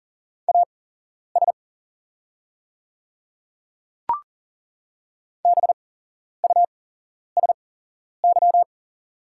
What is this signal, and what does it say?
Morse code, heard as a keyed tone, about 40 wpm, spaced as for 11 wpm, 700 Hz. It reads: AS BUSY
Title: 8x Character Spacing - Top 500 Words - 40wpm